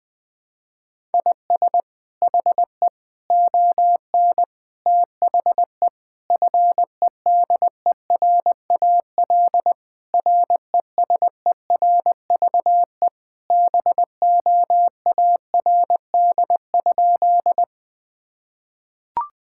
Morse code, heard as a keyed tone, about 21 wpm, 700 Hz. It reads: IS HE ON THE FEDERAL RESERVE BOARD?
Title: Sentences from Top 1000 Words - Review Part 6 - 20wpm